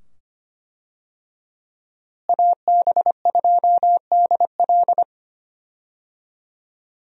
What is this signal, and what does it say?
Morse code, heard as a keyed tone, about 25 wpm, 700 Hz. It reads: AB2DL
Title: USA Call Signs - 5 Characters - Repeated 2x - 25wpm